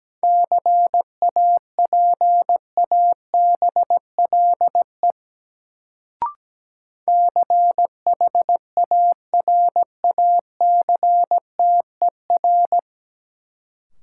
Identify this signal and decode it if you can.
Morse code, heard as a keyed tone, about 17 wpm, 700 Hz. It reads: CAPABLE CHARACTER
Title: Sentences from Top 1200 Words - New Word Drill - 17wpm